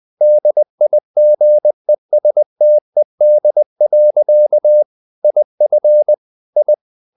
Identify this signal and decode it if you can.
Morse code, heard as a keyed tone, about 20 wpm, 600 Hz. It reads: DIGESTED. IF I
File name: the old man and the sea / the old man and the sea part 1 20wpm